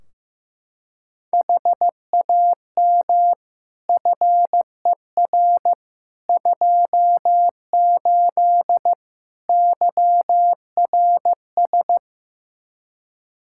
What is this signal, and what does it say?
Morse code, heard as a keyed tone, about 15 wpm, 700 Hz. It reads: HAM FER 28 YRS
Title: Practice QSO - Exchange 3 - More Info - 15wpm